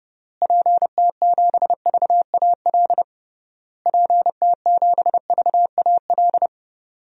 Morse code, heard as a keyed tone, about 32 wpm, 700 Hz. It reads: PT7VAL PT7VAL
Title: DXCC-100 Call Signs - 6 Characters - Repeated 3x - 30wpm